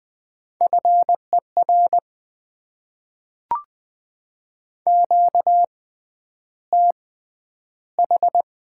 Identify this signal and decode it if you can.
Morse code, heard as a keyed tone, about 20 wpm, spaced as for 10 wpm, 700 Hz. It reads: FER QTH
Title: Top 20 QSO Elements - 6x Character Spacing - 20wpm